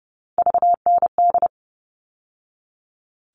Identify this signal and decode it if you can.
Morse code, heard as a keyed tone, about 30 wpm, 700 Hz. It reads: VND
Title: All Cut Numbers - 3-Digit 30wpm